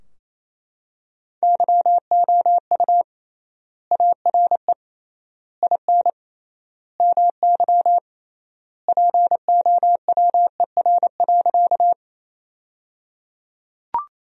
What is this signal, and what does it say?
Morse code, heard as a keyed tone, about 27 wpm, 700 Hz. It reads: YOU ARE IN MY POWER.
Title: 3x Word-Spacing - Sentences from Top 500 Words - 28wpm